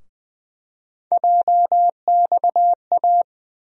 Morse code, heard as a keyed tone, about 20 wpm, 700 Hz. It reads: JXA
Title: USA Call Sign Postfixes - 3 Letters 20wpm